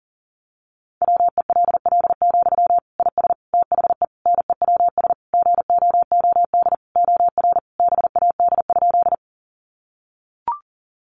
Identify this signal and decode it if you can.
Morse code, heard as a keyed tone, about 40 wpm, 700 Hz. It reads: WELL, IS THE NEWS GOOD OR BAD?
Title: Sentences from Top 800 Words 40wpm